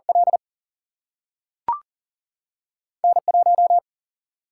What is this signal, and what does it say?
Morse code, heard as a keyed tone, about 40 wpm, 700 Hz. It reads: L N 1
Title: Single-Letter Single-Number 40wpm